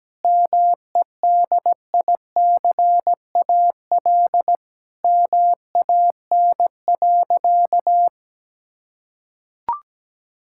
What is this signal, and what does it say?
Morse code, heard as a keyed tone, about 17 wpm, 700 Hz. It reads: MEDICAL MAN.
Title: Sentences from Top 1000 Words - Review Part 5 - 17wpm